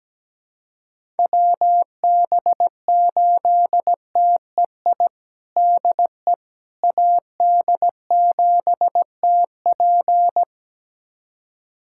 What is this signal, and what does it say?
Morse code, heard as a keyed tone, about 17 wpm, 700 Hz. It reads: WB8TEI DE AD7TP
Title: USA Callsign DE Callsign 17wpm